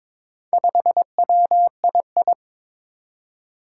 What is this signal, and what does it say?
Morse code, heard as a keyed tone, about 21 wpm, 700 Hz. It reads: 5WII
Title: Four Challenging Letter-Number Combinations 22wpm